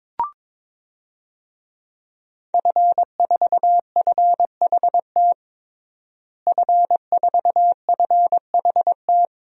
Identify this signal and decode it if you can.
Morse code, heard as a keyed tone, about 23 wpm, 700 Hz. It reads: F4FHT F4FHT
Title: DXCC-100 Call Signs - 5 Characters - Repeated 2x - 22wpm